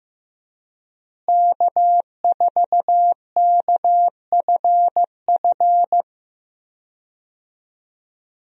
Morse code, heard as a keyed tone, about 15 wpm, 700 Hz. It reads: K4KFF
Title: USA Call Signs - 5 Characters - Repeated 3x - 15wpm